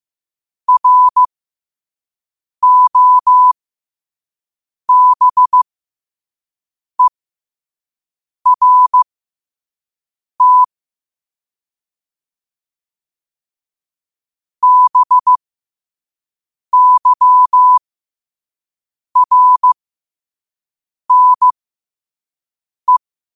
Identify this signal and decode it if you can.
Morse code, heard as a keyed tone, about 15 wpm, spaced as for 5 wpm, 1000 Hz. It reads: ROBERT BYRNE
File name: qotdfiles / 5wpm